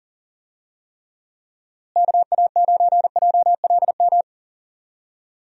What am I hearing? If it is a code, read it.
Morse code, heard as a keyed tone, about 40 wpm, 700 Hz. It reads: KA9JLM
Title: USA Call Signs - 6 Characters - Repeated 2x - 40wpm